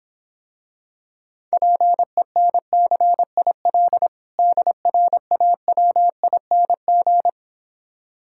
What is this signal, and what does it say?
Morse code, heard as a keyed tone, about 26 wpm, 700 Hz. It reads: PENCIL DRAWING